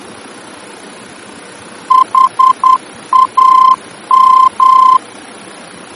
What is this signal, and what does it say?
Morse code, heard as a keyed tone, about 10 wpm, 1040 Hz. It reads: HAM
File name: germany / mx / HAM-D-VOR